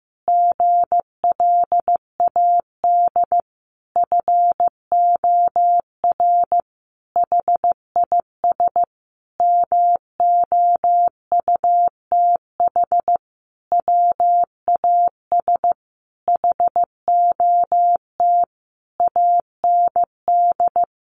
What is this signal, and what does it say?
Morse code, heard as a keyed tone, about 15 wpm, 700 Hz. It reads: GLAD FOR HIS MOUTH WAS HOT AND
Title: Sentences from Top 1500 Words 15wpm